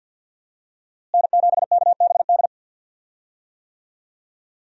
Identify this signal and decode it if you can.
Morse code, heard as a keyed tone, about 52 wpm, 700 Hz. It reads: N7KBD
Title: USA Call Signs - 5 Characters - Repeated 3x - 50wpm